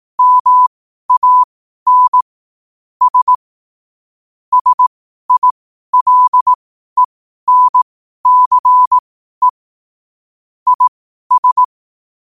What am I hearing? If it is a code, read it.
Morse code, heard as a keyed tone, about 18 wpm, spaced as for 12 wpm, 1000 Hz. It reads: MANS SILENCE IS
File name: qotdfiles / 12wpm